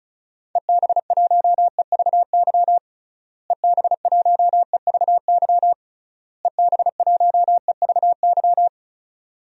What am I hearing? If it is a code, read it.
Morse code, heard as a keyed tone, about 34 wpm, 700 Hz. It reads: EB1EVY EB1EVY EB1EVY
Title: DXCC-100 Call Signs - 6 Characters - Repeated 3x - 35wpm